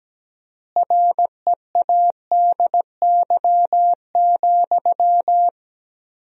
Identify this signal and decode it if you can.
Morse code, heard as a keyed tone, about 17 wpm, 700 Hz. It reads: READY,
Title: 4x Word-Spacing - Sentences from Top 500 Words - 17wpm